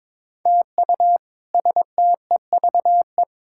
Morse code, heard as a keyed tone, about 22 wpm, 700 Hz. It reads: TU STEVE
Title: K1USN SST QSO Practice - 22wpm